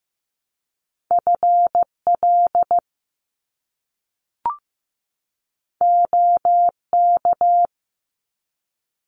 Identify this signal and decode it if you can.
Morse code, heard as a keyed tone, about 15 wpm, 700 Hz. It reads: FL OK
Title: US States 15wpm